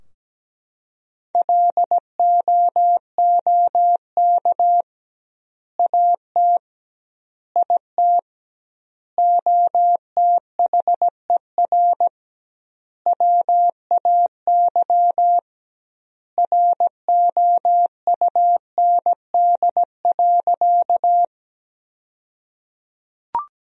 Morse code, heard as a keyed tone, about 17 wpm, 700 Hz. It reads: LOOK AT IT OTHER WAY ROUND.
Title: 2x Word-Spacing - Sentences from Top 200 Words - 17wpm